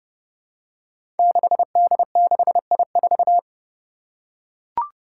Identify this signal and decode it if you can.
Morse code, heard as a keyed tone, about 31 wpm, 700 Hz. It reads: 6D6I4
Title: Five Challenging Letter-Number Combinations 30wpm